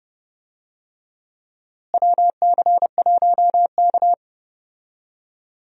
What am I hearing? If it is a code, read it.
Morse code, heard as a keyed tone, about 30 wpm, 700 Hz. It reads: WC1K
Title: USA Call Signs - 4 Characters - Repeated 2x - 30wpm